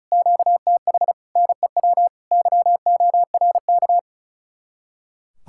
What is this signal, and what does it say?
Morse code, heard as a keyed tone, about 46 wpm, spaced as for 41 wpm, 700 Hz. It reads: QTH NEW YORK
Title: Speed-Racing Common QSO Phrases - 35wpm